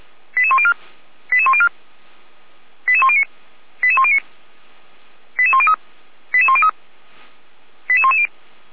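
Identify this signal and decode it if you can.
Five-tone selective call (ZVEI), heard as quick runs of five tones, tones of about 70 ms. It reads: 88104 88104 88109 88109 88103 88103 88110
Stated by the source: ZVEI five-tone call 88383 received by SDR.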